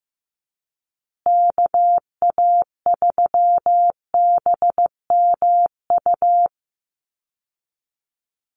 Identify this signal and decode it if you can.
Morse code, heard as a keyed tone, about 15 wpm, 700 Hz. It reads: KA3BMU
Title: Call Signs Repeated 2x - 15wpm